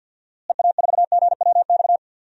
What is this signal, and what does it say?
Morse code, heard as a keyed tone, about 47 wpm, 700 Hz. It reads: EA4GWX
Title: DXCC-100 Call Signs - 6 Characters - Repeated 2x - 50wpm